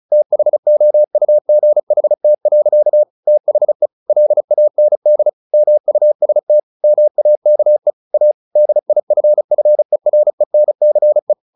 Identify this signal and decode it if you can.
Morse code, heard as a keyed tone, about 35 wpm, 600 Hz. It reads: THOUGHT. THE LAND MUST MAKE A DIFFERENCE
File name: the old man and the sea / the old man and the sea 1 35wpm